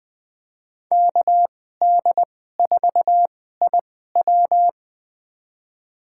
Morse code, heard as a keyed tone, about 20 wpm, spaced as for 14 wpm, 700 Hz. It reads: KD4IW
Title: USA Call Signs - 5 Characters - 2x Character Spacing - 20wpm